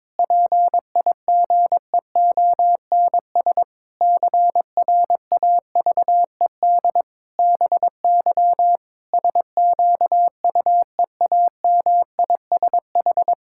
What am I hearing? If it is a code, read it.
Morse code, heard as a keyed tone, about 22 wpm, 700 Hz. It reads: PIGEONS CRAVED BY SQUEAMISH
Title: Pangram Challenge v2 - 22wpm